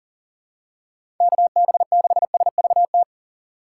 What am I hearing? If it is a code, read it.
Morse code, heard as a keyed tone, about 40 wpm, 700 Hz. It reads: KB6SVT
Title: USA Call Signs - 6 Characters - 4x Character Spacing - 40wpm